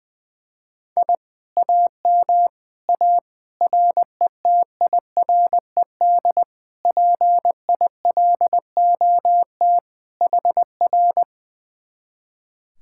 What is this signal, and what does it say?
Morse code, heard as a keyed tone, about 21 wpm, 700 Hz. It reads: I AM A RETIRED PILOT HR
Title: Speed-Racing Common QSO Phrases - 20wpm